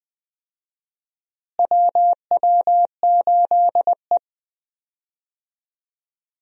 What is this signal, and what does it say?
Morse code, heard as a keyed tone, about 20 wpm, 700 Hz. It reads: WW8E
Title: USA Call Signs - 4 Characters - Repeated 2x - 20wpm